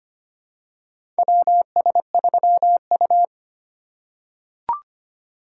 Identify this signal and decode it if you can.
Morse code, heard as a keyed tone, about 25 wpm, 700 Hz. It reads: WS3U
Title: USA Call Signs - 4 Characters - 4x Character Spacing - 25wpm